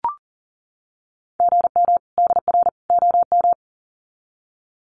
Morse code, heard as a keyed tone, about 40 wpm, 700 Hz. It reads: GM DR OM
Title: Common QSO Phrases 40wpm